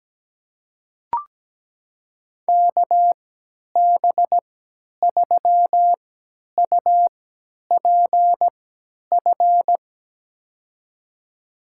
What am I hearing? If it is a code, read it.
Morse code, heard as a keyed tone, about 17 wpm, spaced as for 10 wpm, 700 Hz. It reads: KB3UPF
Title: USA Call Signs - 3x Character Spacing - 17wpm